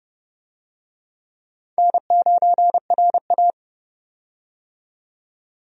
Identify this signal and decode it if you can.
Morse code, heard as a keyed tone, about 30 wpm, 700 Hz. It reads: N9RA
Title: USA Call Signs - 4 Characters - 30wpm